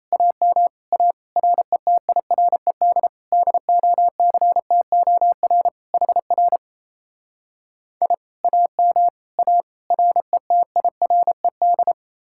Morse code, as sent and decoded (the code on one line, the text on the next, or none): .- -- / .- / .-. - .. .-. -.. / -.. --- -.-. - --- .-. / .... .-. / .. / .- -- / .- / .-. - .. .-. -..
AM A RETIRED DOCTOR HR I AM A RETIRED